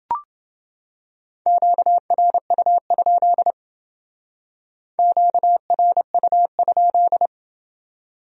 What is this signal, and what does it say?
Morse code, heard as a keyed tone, about 28 wpm, 700 Hz. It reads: QRU? QRU?